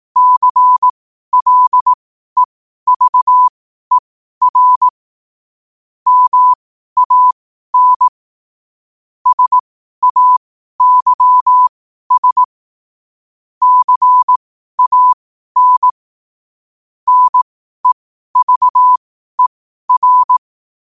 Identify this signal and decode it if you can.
Morse code, heard as a keyed tone, about 18 wpm, spaced as for 12 wpm, 1000 Hz. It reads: CLEVER MAN SAYS CAN NEVER
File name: qotdfiles / 12wpm